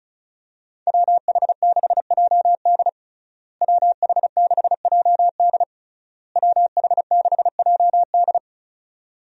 Morse code, heard as a keyed tone, about 35 wpm, 700 Hz. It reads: WH6JD WH6JD WH6JD
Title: USA Call Signs - 5 Characters - Repeated 3x - 35wpm